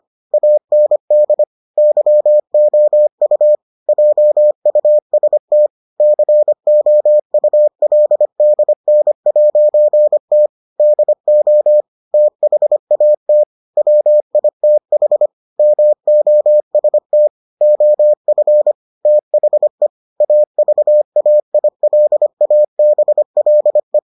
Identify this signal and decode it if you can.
Morse code, heard as a keyed tone, about 25 wpm, 600 Hz. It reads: AND YOU JUST COULDN'T DO THAT WITH MOST OF THE AVAILABLE